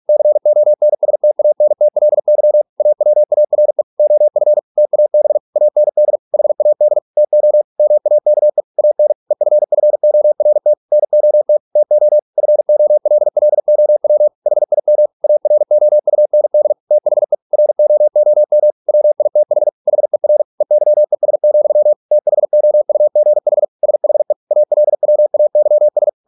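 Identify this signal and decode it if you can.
Morse code, heard as a keyed tone, about 46 wpm, 600 Hz. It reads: CONSTANTLY AWARE OF TAB AND HAD TO MAKE AN EFFORT NOT TO FOLLOW HIM AROUND THE ROOM WITH HER EYES, THOUGH SHE ALWAYS
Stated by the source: Morse practice at 45 wpm, the novel Never by Ken Follett in English, from CW med Gnister.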